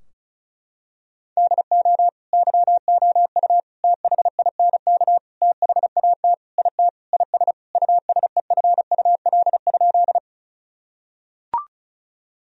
Morse code, heard as a keyed tone, about 36 wpm, 700 Hz. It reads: DO YOU THINK THAT IT IS USEFUL?
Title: Sentences from Top 1000 Words - Review Part 4 - 35wpm